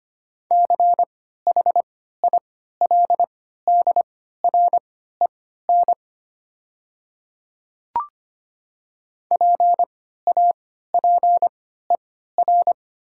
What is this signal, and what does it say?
Morse code, heard as a keyed tone, about 25 wpm, spaced as for 14 wpm, 700 Hz. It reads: CHILDREN PAPER
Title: 3x Character Spacing - Top 300 Words - 25wpm